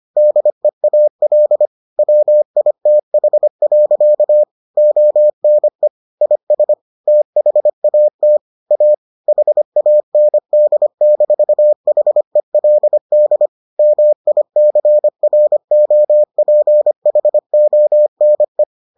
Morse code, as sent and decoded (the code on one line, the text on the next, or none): -.. .- .-.. / .-- .. - .... .-.-.- / --- -. / .. ... / - .... .- - / .- / .... .- -. -.. -....- .... .-.. -.. / -- .. -.-. .-. --- .--. .... --- -. .
DEAL WITH. ONE IS THAT A HAND-HELD MICROPHONE